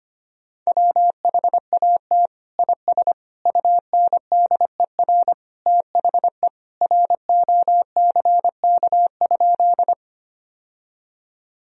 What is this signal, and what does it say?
Morse code, heard as a keyed tone, about 25 wpm, 700 Hz. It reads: WHAT IS UNDER THE ROCK?
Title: Sentences from Top 1000 Words - Review Part 2 - 25wpm